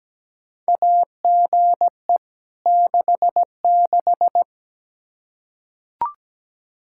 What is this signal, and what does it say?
Morse code, heard as a keyed tone, about 17 wpm, 700 Hz. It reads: AGE 66
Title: K1USN SST QSO Practice 17wpm